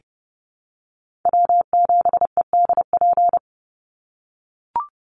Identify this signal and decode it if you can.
Morse code, heard as a keyed tone, about 30 wpm, 700 Hz. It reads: W7EDP
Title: USA Call Signs 30wpm